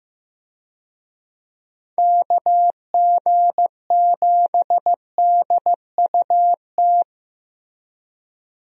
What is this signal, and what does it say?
Morse code, heard as a keyed tone, about 15 wpm, 700 Hz. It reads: KG7DUT
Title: USA Call Signs - 6 Characters - Repeated 3x - 15wpm